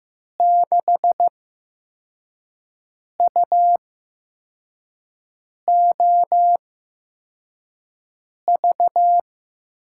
Morse code, heard as a keyed tone, about 15 wpm, spaced as for 4 wpm, 700 Hz. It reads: 6UOV
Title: USA Call Signs - 6 Characters - 8x Character Spacing - 15wpm